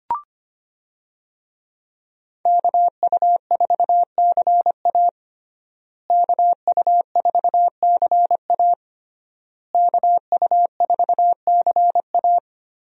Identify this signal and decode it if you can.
Morse code, heard as a keyed tone, about 25 wpm, 700 Hz. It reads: KU4CA KU4CA KU4CA
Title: USA Call Signs - 5 Characters - Repeated 3x - 25wpm